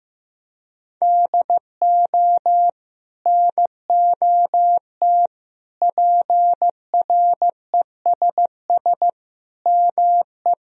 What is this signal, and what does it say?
Morse code, heard as a keyed tone, about 15 wpm, 700 Hz. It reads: DO NOT PRESS ME
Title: Sentences from Top 1000 Words - Review Part 1 - 15wpm